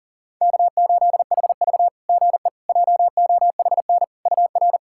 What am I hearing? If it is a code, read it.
Morse code, heard as a keyed tone, about 42 wpm, 690 Hz. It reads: K8HV GE JOHN UR
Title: SOTA QSO Practice 40wpm